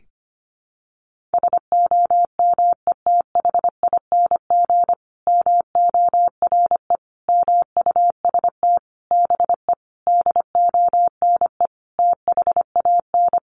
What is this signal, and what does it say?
Morse code, heard as a keyed tone, about 25 wpm, 700 Hz. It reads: SOMETHING MORE MUST BE DONE THAN
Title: Sentences from Top 1400 Words 25wpm